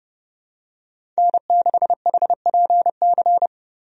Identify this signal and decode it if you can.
Morse code, heard as a keyed tone, about 32 wpm, 700 Hz. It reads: N6HPC
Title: USA Call Signs - 5 Characters - 30wpm